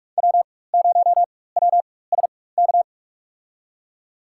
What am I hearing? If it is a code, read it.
Morse code, heard as a keyed tone, about 42 wpm, spaced as for 20 wpm, 690 Hz. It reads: W0WSK
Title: USA Call Signs - 5 Characters - 4x Character Spacing - 45wpm